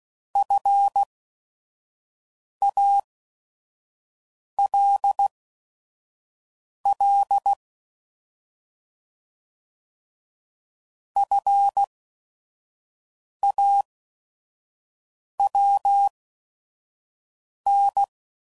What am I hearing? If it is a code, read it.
Morse code, heard as a keyed tone, about 16 wpm, spaced as for 5 wpm, 800 Hz. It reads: FALL FAWN
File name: Prefixes / By First Letter / Fs